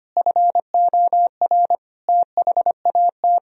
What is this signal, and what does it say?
Morse code, heard as a keyed tone, about 25 wpm, 700 Hz. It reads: FOR THAT
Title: Sentences from Top 1000 Words - Review Part 3 - 25wpm